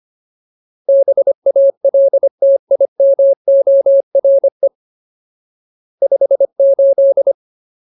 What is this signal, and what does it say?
Morse code, heard as a keyed tone, about 25 wpm, 550 Hz. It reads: BALTIMORE 58
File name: Ultimate Word List / 25WPM Ultimate